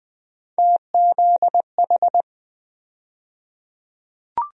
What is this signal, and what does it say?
Morse code, heard as a keyed tone, about 20 wpm, 700 Hz. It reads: TZH
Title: USA Call Sign Postfixes - 3 Letters 20wpm